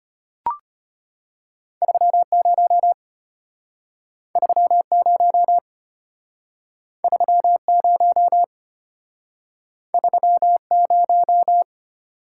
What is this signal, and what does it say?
Morse code, heard as a keyed tone, about 34 wpm, 700 Hz. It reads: 30 30 30 30